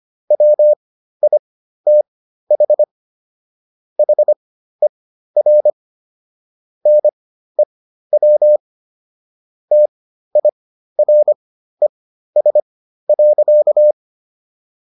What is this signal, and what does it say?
Morse code, heard as a keyed tone, about 25 wpm, spaced as for 13 wpm, 600 Hz. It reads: WITH HER NEW TIRES.